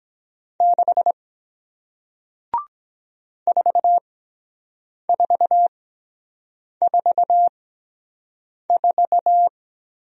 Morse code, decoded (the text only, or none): 6 4 4 4 4